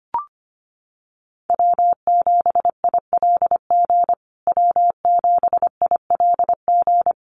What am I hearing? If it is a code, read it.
Morse code, heard as a keyed tone, about 25 wpm, 700 Hz. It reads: W7ILG W7ILG
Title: USA Call Signs Repeated 3x 25wpm